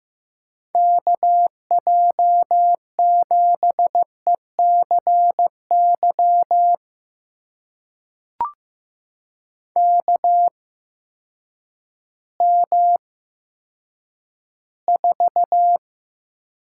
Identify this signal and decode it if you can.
Morse code, heard as a keyed tone, about 15 wpm, spaced as for 9 wpm, 700 Hz. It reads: KJ7ECY KM4
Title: USA Call Signs - 6 Characters - 8x Character Spacing - 15wpm